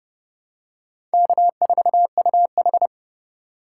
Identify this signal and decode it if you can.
Morse code, heard as a keyed tone, about 32 wpm, 700 Hz. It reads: K4UH
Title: USA Call Signs - 4 Characters - 3x Character Spacing - 30wpm